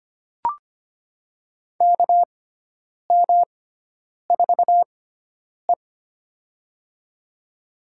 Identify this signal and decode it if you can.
Morse code, heard as a keyed tone, about 25 wpm, spaced as for 9 wpm, 700 Hz. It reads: KM4E
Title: USA Call Signs - 4 Characters - 6x Character Spacing - 25wpm